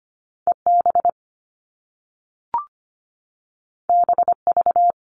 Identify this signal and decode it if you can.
Morse code, heard as a keyed tone, about 25 wpm, 700 Hz. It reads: EB BV
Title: All Cut Numbers - 2-Digit 25wpm